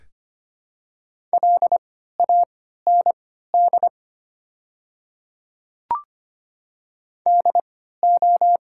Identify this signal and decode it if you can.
Morse code, heard as a keyed tone, about 25 wpm, spaced as for 14 wpm, 700 Hz. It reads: LAND DO
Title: 3x Character Spacing - Top 200 Words - 25wpm